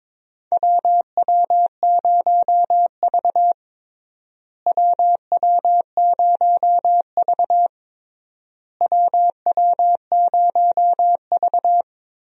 Morse code, heard as a keyed tone, about 22 wpm, 700 Hz. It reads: WW0V WW0V WW0V
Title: USA Call Signs - 4 Characters - Repeated 3x - 22wpm